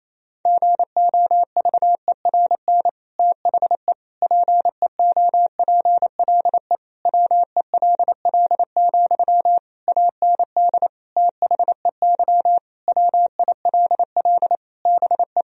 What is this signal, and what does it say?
Morse code, heard as a keyed tone, about 28 wpm, 700 Hz. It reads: GOVERN THE PEOPLE WELL, AND THEY WILL BE
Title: Sentences from Top 1000 Words - Review Part 2 - 28wpm